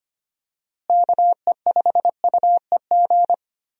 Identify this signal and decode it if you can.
Morse code, heard as a keyed tone, about 25 wpm, 700 Hz. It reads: KE5UEG
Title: USA Call Signs - 6 Characters - Repeated 2x - 25wpm